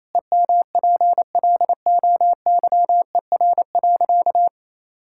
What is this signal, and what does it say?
Morse code, heard as a keyed tone, about 28 wpm, 700 Hz. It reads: EMPLOYER.
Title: Sentences from Top 1600 Words 28wpm